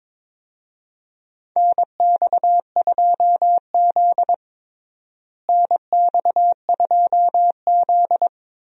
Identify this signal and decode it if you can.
Morse code, heard as a keyed tone, about 21 wpm, 700 Hz. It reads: NX2Z NX2Z
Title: USA Call Signs - 4 Characters - Repeated 3x - 22wpm